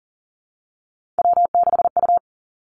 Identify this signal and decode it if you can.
Morse code, heard as a keyed tone, about 38 wpm, 690 Hz. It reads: W6U
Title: Three Challenging Letter-Number Combinations 40wpm